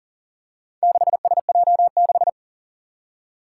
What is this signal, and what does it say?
Morse code, heard as a keyed tone, about 40 wpm, 700 Hz. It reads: 6SJ6
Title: Four Challenging Letter-Number Combinations 40wpm